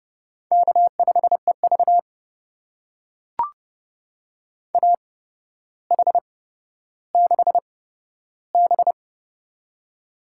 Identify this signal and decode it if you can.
Morse code, heard as a keyed tone, about 29 wpm, spaced as for 13 wpm, 700 Hz. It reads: K5EV AH6B
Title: USA Call Signs - 4 Characters - 8x Character Spacing - 30wpm